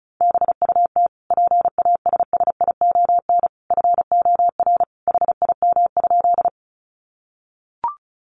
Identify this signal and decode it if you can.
Morse code, heard as a keyed tone, about 34 wpm, 700 Hz. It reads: BUT PASSION FOR HIM?
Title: Sentences from Top 1300 Words 35wpm